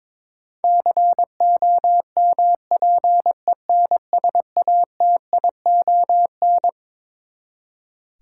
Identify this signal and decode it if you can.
Morse code, heard as a keyed tone, about 22 wpm, 700 Hz. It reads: COMPENSATION
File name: Twelve Letter Words - 22wpm